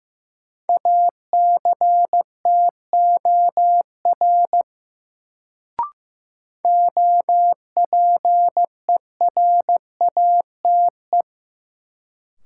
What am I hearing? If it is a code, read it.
Morse code, heard as a keyed tone, about 15 wpm, 700 Hz. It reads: ACTOR OPERATE
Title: Sentences from Top 1500 Words - New Word Drill Alt - 15wpm